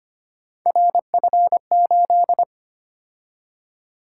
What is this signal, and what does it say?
Morse code, heard as a keyed tone, about 25 wpm, 700 Hz. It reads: RF8
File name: Three Letter-Number Combinations - 25wpm